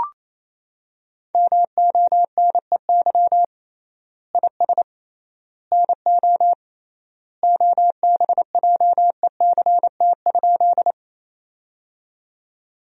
Morse code, heard as a keyed tone, about 28 wpm, 700 Hz. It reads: MONEY IS NO OBJECT?
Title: 3x Word-Spacing - Sentences from Top 500 Words - 28wpm